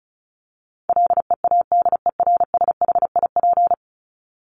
Morse code, heard as a keyed tone, about 34 wpm, 700 Hz. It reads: LEADERSHIP
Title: Top 1501-1750 Words 35wpm